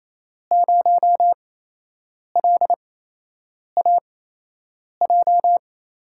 Morse code, heard as a keyed tone, about 28 wpm, spaced as for 8 wpm, 700 Hz. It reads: 0LAJ
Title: USA Call Signs - 6 Characters - 8x Character Spacing - 28wpm